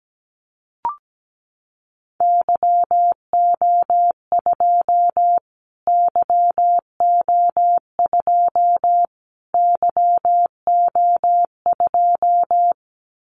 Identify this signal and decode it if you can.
Morse code, heard as a keyed tone, about 17 wpm, 700 Hz. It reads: YO2 YO2 YO2
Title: Callsign Prefixes for Top 25 DXCC - Repeated 3x - 17wpm